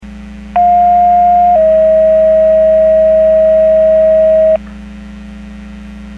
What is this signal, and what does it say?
Two-tone page, heard as a one-second tone, then a three-second tone, about 710 then 640 Hz. It is heard from 0.6 to 4.6 s.
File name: 03754 21-11 Portage 154.385 Paging